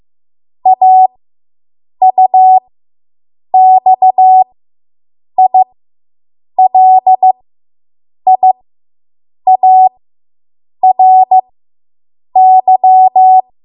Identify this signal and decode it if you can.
Morse code, heard as a keyed tone, about 15 wpm, spaced as for 7 wpm, 750 Hz. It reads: AUXILIARY